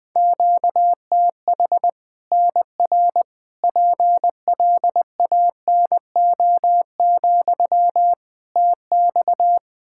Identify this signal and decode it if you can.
Morse code, heard as a keyed tone, about 21 wpm, 700 Hz. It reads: QTH NR PLANO, TX
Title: Practice QSO - Exchange 1 - Beginning - 20wpm